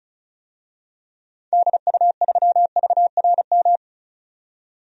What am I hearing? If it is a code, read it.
Morse code, heard as a keyed tone, about 34 wpm, 700 Hz. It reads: DU3VRM
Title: DXCC-100 Call Signs - 6 Characters - 35wpm